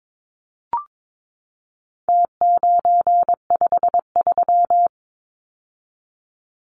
Morse code, heard as a keyed tone, about 22 wpm, 700 Hz. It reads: T953
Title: Numbers - Contest Style - 4-Digit 22wpm